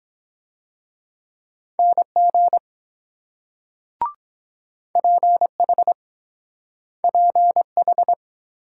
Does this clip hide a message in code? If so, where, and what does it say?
Yes: Morse code from 1.8 to 8.1 s, heard as a keyed tone, about 23 wpm, 700 Hz. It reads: NG PH PH